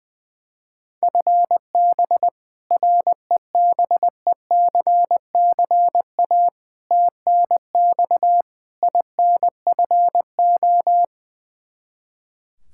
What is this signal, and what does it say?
Morse code, heard as a keyed tone, about 20 wpm, 700 Hz. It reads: FB REBECCA TNX INFO
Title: Practice QSO - Exchange 3 - More Info - 20wpm